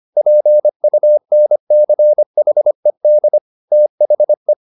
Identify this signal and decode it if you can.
Morse code, heard as a keyed tone, about 25 wpm, 600 Hz. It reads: PUNCHED THE